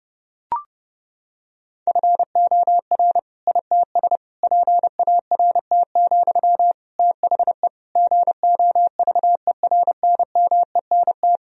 Morse code, heard as a keyed tone, about 32 wpm, 700 Hz. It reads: FOR ITS PART, THE GOVERNMENT
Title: Sentences from Top 1000 Words - Review Part 3 - 30wpm